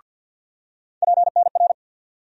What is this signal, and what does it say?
Morse code, heard as a keyed tone, about 47 wpm, 700 Hz. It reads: PNR
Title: USA Call Sign Postfixes - 3 Letters 50wpm